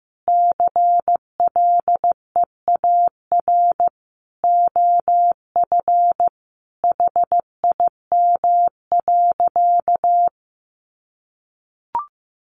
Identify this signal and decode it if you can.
Morse code, heard as a keyed tone, about 15 wpm, 700 Hz. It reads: CLEAR OF HIM.
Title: Sentences from Top 500 Words 15wpm